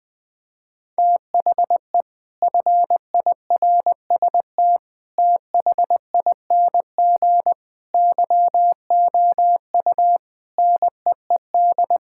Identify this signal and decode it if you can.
Morse code, heard as a keyed tone, about 20 wpm, 700 Hz. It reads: THE FIRST THING YOU NEED
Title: Sets of 8 Words - 20wpm